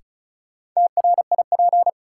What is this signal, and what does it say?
Morse code, heard as a keyed tone, about 34 wpm, 700 Hz. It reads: TRIP
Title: Sentences from Top 1000 Words - Review Part 4 - New Word Drill - 35wpm